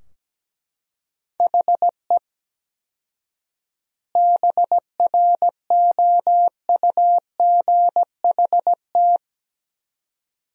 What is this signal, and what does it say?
Morse code, heard as a keyed tone, about 17 wpm, 700 Hz. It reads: HE BROUGHT
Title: 4x Word-Spacing - Sentences from Top 500 Words - 17wpm